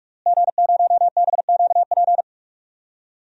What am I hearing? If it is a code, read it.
Morse code, heard as a keyed tone, about 46 wpm, 700 Hz. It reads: G0BQP
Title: DXCC-100 Call Signs - 5 Characters - Repeated 3x - 45wpm